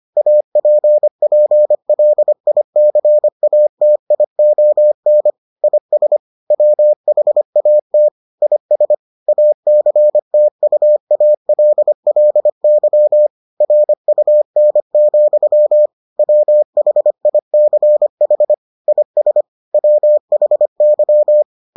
Morse code, heard as a keyed tone, about 25 wpm, 600 Hz. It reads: APPLICATION IS WHAT IS ACTUALLY RUN, WHICH IS WHY